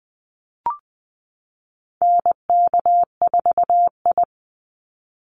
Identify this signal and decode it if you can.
Morse code, heard as a keyed tone, about 20 wpm, 700 Hz. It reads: NK4I
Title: Call Signs Repeated 2x - 20wpm